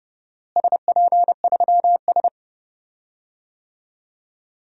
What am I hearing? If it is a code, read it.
Morse code, heard as a keyed tone, about 32 wpm, 700 Hz. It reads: SP3S
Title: DXCC-100 Call Signs - 4 Characters - Repeated 2x - 30wpm